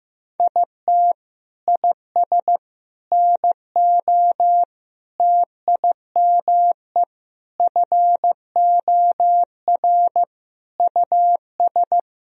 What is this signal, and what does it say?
Morse code, heard as a keyed tone, about 15 wpm, 700 Hz. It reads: IT IS NO TIME FOR US
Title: Sentences from Top 1000 Words - Review Part 3 - 15wpm